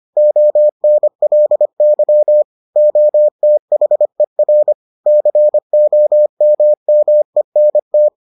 Morse code, heard as a keyed tone, about 25 wpm, 600 Hz. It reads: ONLY OTHER COMMENT